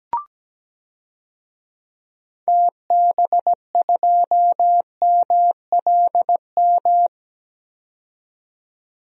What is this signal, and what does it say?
Morse code, heard as a keyed tone, about 17 wpm, 700 Hz. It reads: TB2MLM
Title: DXCC-100 Call Signs - 6 Characters - 17wpm